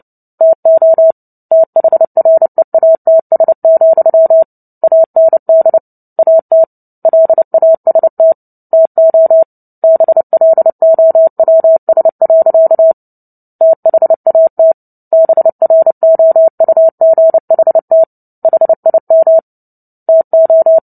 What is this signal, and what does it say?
Morse code, heard as a keyed tone, about 30 wpm, 650 Hz. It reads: TO THREATS, AND AT LAST TO BLOWS. THAT BROUGHT HIM TO
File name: Book Two / 03-Chapter 3 - 29 WPM